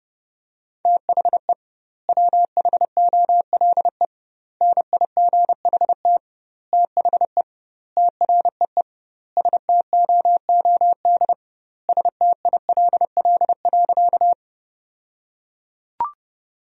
Morse code, heard as a keyed tone, about 32 wpm, 700 Hz. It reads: THE WHOLE NIGHT THE TREE STOOD STILL.
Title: 2x Word-Spacing - Sentences from Top 500 Words - 30wpm